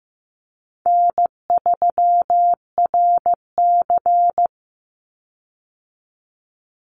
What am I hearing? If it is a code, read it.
Morse code, heard as a keyed tone, about 15 wpm, 700 Hz. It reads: N3RC
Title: Call Signs Repeated 3x - 15wpm